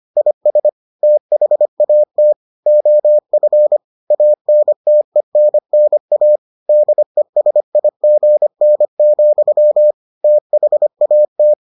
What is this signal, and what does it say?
Morse code, heard as a keyed tone, about 25 wpm, 600 Hz. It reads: IS THAT OF ANTENNA DESIGN, THAT